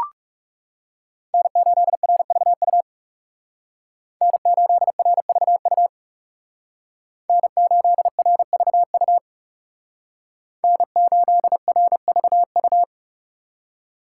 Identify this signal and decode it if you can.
Morse code, heard as a keyed tone, about 37 wpm, 700 Hz. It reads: N8RVU N8RVU N8RVU N8RVU
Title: Speed Racing - USA 1x3 Call Signs 30wpm